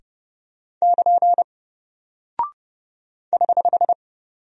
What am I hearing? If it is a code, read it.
Morse code, heard as a keyed tone, about 30 wpm, 700 Hz. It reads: <KN> <HH>
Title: Single Character (Letter:Number:Punctuation:Prosign) Alt 30wpm